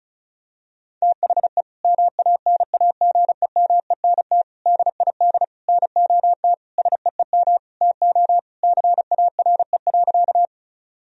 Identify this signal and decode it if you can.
Morse code, heard as a keyed tone, about 34 wpm, 700 Hz. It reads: THE MANAGEMENT DID NOT SEEM TO CARE.
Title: Sentences from Top 1000 Words - Review Part 4 - 35wpm